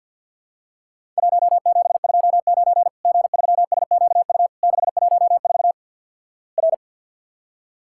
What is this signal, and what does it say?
Morse code, heard as a keyed tone, about 50 wpm, 700 Hz. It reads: 1729 G3SQU 614 R
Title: Worked All Europe DX Contest - QTC Exchange with Repeats - 50wpm